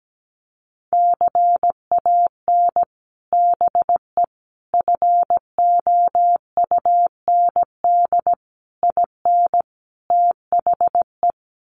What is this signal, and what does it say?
Morse code, heard as a keyed tone, about 17 wpm, 700 Hz. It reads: CAN BE FOUND IN THE
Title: Sets of 5 Words Encore 17wpm